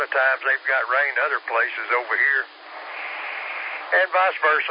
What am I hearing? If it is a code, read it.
Emergency Alert System (SAME) tones, heard as harsh buzzing bursts, with speech. It is heard from 0.9 to 3.8 s.